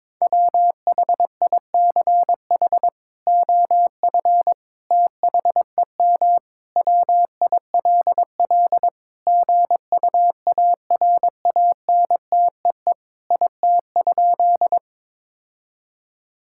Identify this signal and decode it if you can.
Morse code, heard as a keyed tone, about 22 wpm, 700 Hz. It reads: WHICH OF THEM WILL GUARANTEE IT?
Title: Sentences from Top 1600 Words 22wpm